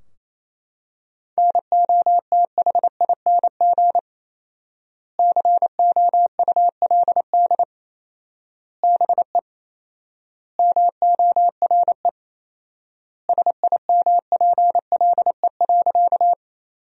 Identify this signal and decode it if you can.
Morse code, heard as a keyed tone, about 28 wpm, 700 Hz. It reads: NOTHING COULD BE MORE SIMPLE.
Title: 4x Word-Spacing - Sentences from Top 500 Words - 28wpm